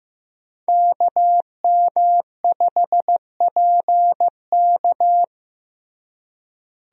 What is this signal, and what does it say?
Morse code, heard as a keyed tone, about 15 wpm, 700 Hz. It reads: KM5PK
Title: USA Call Signs - 5 Characters - Repeated 3x - 15wpm